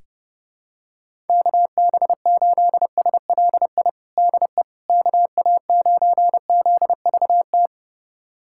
Morse code, heard as a keyed tone, about 29 wpm, 700 Hz. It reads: KB8SLI DE KA9ZVT
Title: USA Callsign DE Callsign - 6 Characters - 30wpm